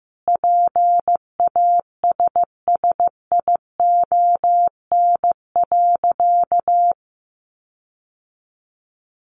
Morse code, heard as a keyed tone, about 15 wpm, 700 Hz. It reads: PASSION.
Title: Sentences from Top 1300 Words 15wpm